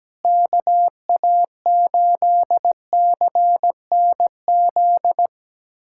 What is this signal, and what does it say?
Morse code, heard as a keyed tone, about 17 wpm, 700 Hz. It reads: KA8CNZ
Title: USA Callsign DE Callsign 17wpm